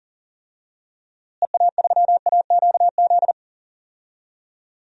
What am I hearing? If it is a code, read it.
Morse code, heard as a keyed tone, about 40 wpm, 700 Hz. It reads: EA3AQZ
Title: DXCC-100 Call Signs - 6 Characters - Repeated 2x - 40wpm